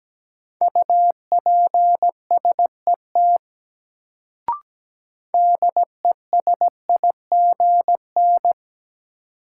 Morse code, heard as a keyed tone, about 17 wpm, 700 Hz. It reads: UPSET DESIGN
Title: Sentences from Top 800 Words - New Word Drill - 17wpm